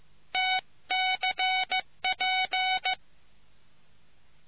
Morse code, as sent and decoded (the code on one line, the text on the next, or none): - -.-. .--.
TCP